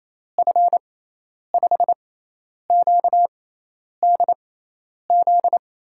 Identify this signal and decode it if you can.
Morse code, heard as a keyed tone, about 28 wpm, spaced as for 10 wpm, 700 Hz. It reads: F5QDZ
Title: USA Call Signs - 6 Characters - 6x Character Spacing - 28wpm